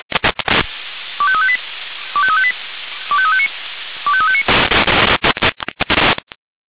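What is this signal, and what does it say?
Five-tone selective call (ZVEI), heard as quick runs of five tones, tones of about 70 ms. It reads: 25358 25358 25359 25359
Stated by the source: ZVEI five-tone call 25343 received by SDR.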